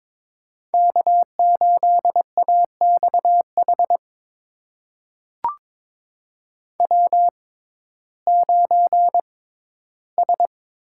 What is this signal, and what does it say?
Morse code, heard as a keyed tone, about 21 wpm, spaced as for 13 wpm, 700 Hz. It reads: K8AXH W9S
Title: USA Call Signs - 5 Characters - 6x Character Spacing - 22wpm